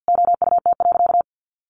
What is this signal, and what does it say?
Morse code, heard as a keyed tone, about 50 wpm, 700 Hz. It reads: OUT.
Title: Sentences from Top 1400 Words 50wpm